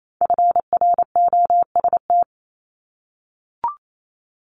Morse code, heard as a keyed tone, about 28 wpm, 700 Hz. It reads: FROST
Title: Multi-letter Phonemes Repeated 3x 28wpm